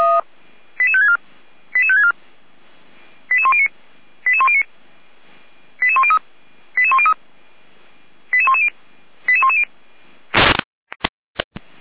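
Five-tone selective call (ZVEI), heard as quick runs of five tones, tones of about 70 ms. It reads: 88564 88564 88109 88109 88103 88103 88110 88110
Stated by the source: Five-tone ZVEI selective call received on an SDR, from the feuerwehr-orlamuende.org fire brigade server.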